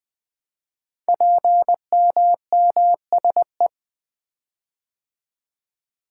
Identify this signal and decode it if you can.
Morse code, heard as a keyed tone, about 20 wpm, 700 Hz. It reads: PMMSE
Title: Code Groups - 5 Letters - 20wpm